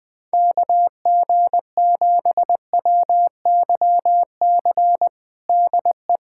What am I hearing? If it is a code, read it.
Morse code, heard as a keyed tone, about 20 wpm, 700 Hz. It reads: KG7WYC DE
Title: USA Callsign DE Callsign 20wpm